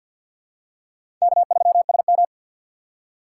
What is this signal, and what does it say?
Morse code, heard as a keyed tone, about 52 wpm, 700 Hz. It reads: K3SM
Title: USA Call Signs - 4 Characters - 3x Character Spacing - 50wpm